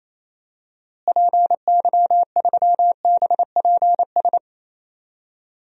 Morse code, heard as a keyed tone, about 28 wpm, 700 Hz. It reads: PY3BPS
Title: DXCC-100 Call Signs - 6 Characters - Repeated 2x - 28wpm